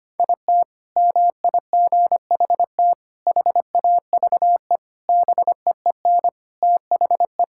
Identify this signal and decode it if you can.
Morse code, heard as a keyed tone, about 25 wpm, 700 Hz. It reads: IT MIGHT HAVE BEEN THE
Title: Sentences from Top 1600 Words 25wpm